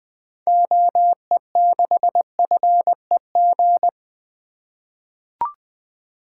Morse code, heard as a keyed tone, about 21 wpm, 700 Hz. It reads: OE6FEG
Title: 2023 SOTA Activator Call Signs - 20wpm